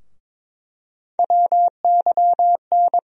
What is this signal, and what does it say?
Morse code, heard as a keyed tone, about 21 wpm, 700 Hz. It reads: WYN
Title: USA Call Sign Postfixes - 3 Letters 22wpm